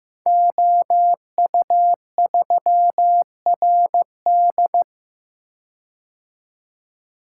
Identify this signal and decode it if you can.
Morse code, heard as a keyed tone, about 15 wpm, 700 Hz. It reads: OU3RD
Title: Code Groups - 5 Letter:Numbers - 15wpm